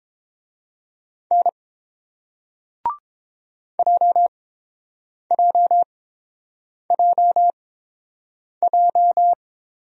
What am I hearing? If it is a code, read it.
Morse code, heard as a keyed tone, about 30 wpm, spaced as for 7 wpm, 700 Hz. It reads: N JJJJ